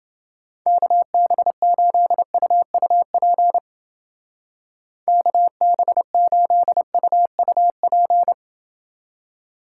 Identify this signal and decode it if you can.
Morse code, heard as a keyed tone, about 30 wpm, 700 Hz. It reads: KB8UUP KB8UUP